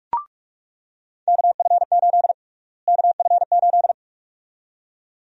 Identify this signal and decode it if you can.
Morse code, heard as a keyed tone, about 44 wpm, 700 Hz. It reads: KF8 KF8
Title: USA Call Sign Prefixes - Repeated 2x - 45wpm